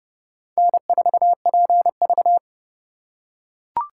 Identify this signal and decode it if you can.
Morse code, heard as a keyed tone, about 30 wpm, 700 Hz. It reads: N4PV